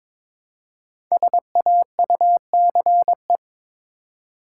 Morse code, heard as a keyed tone, about 22 wpm, 700 Hz. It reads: SAUCE
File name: 5 Letter Words - First 3 Letters Given - 22wpm